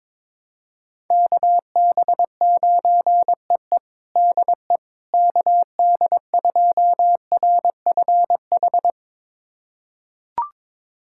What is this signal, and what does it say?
Morse code, heard as a keyed tone, about 21 wpm, 700 Hz. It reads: KB9EE DE KD2RFH
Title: USA Callsign DE Callsign 22wpm